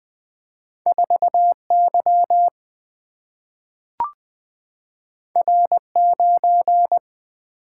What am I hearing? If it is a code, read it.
Morse code, heard as a keyed tone, about 20 wpm, 700 Hz. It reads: 4 Y R 9
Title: Single-Letter Single-Number 20wpm